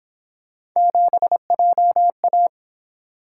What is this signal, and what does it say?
Morse code, heard as a keyed tone, about 26 wpm, 700 Hz. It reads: 7JA
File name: Speed-Racing - Three Letter-Number Combinations - 22wpm